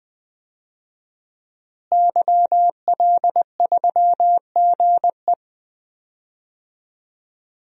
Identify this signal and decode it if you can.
Morse code, heard as a keyed tone, about 20 wpm, 700 Hz. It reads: YL3GE
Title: DXCC-100 Call Signs - 5 Characters - Repeated 3x - 20wpm